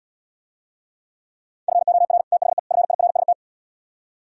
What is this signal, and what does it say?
Morse code, heard as a keyed tone, about 76 wpm, 700 Hz. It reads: FOR THE FIRST